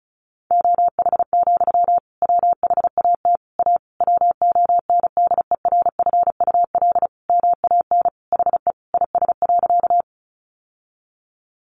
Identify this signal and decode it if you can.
Morse code, heard as a keyed tone, about 34 wpm, 700 Hz. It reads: OH, WHAT A WONDERFUL MAN HE IS.
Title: Sentences from Top 1100 Words 35wpm